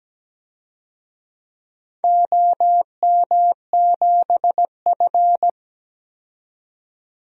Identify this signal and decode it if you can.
Morse code, heard as a keyed tone, about 17 wpm, 700 Hz. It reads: OM7F
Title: DXCC-100 Call Signs - 4 Characters - 17wpm